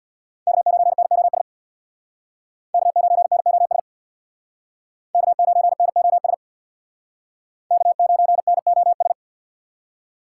Morse code, heard as a keyed tone, about 62 wpm, 700 Hz. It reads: K9NOS K9NOS K9NOS K9NOS
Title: Speed Racing - USA 1x3 Call Signs 50wpm